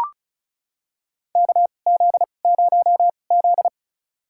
Morse code, heard as a keyed tone, about 34 wpm, spaced as for 25 wpm, 700 Hz. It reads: KZ0Z
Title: USA Call Signs - 4 Characters - 2x Character Spacing - 35wpm